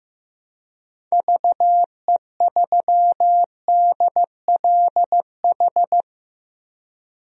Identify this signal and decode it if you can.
Morse code, heard as a keyed tone, about 15 wpm, 700 Hz. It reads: VE3DLH
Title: DXCC-100 Call Signs - 6 Characters - Repeated 2x - 15wpm